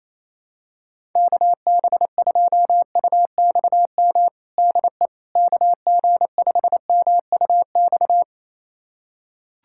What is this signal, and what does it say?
Morse code, heard as a keyed tone, about 28 wpm, 700 Hz. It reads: KB2UXM DE KG5MUX
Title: USA Callsign DE Callsign - 6 Characters - 28wpm